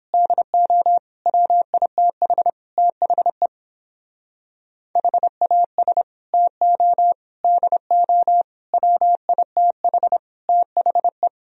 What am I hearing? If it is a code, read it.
Morse code, heard as a keyed tone, about 31 wpm, 700 Hz. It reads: DO WITH THE HAS TO DO WITH THE